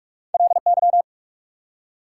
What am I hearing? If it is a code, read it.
Morse code, heard as a keyed tone, about 44 wpm, 690 Hz. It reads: LY
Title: Prefixes and Suffixes 45wpm